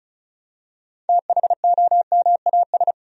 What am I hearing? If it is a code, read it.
Morse code, heard as a keyed tone, about 36 wpm, 700 Hz. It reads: THOMAS
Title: USA Common Names - 35wpm